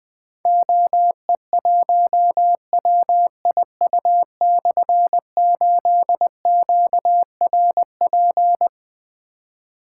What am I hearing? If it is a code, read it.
Morse code, heard as a keyed tone, about 20 wpm, 700 Hz. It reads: OE1WIU/8QRP
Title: Difficult Call Signs 20wpm